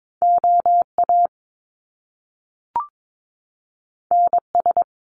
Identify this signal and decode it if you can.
Morse code, heard as a keyed tone, about 22 wpm, 700 Hz. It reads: OA NS
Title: Top Two Letter Combinations in Words 22wpm